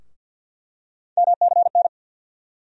Morse code, heard as a keyed tone, about 51 wpm, 700 Hz. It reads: MCN
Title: USA Call Sign Postfixes - 3 Letters 50wpm